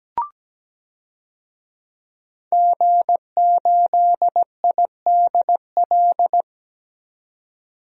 Morse code, heard as a keyed tone, about 17 wpm, 700 Hz. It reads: G8IDL
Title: DXCC-100 Call Signs - 5 Characters - 17wpm